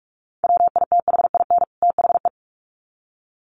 Morse code, heard as a keyed tone, about 44 wpm, 700 Hz. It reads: WITHIN THE
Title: Sets of 2 Words 45wpm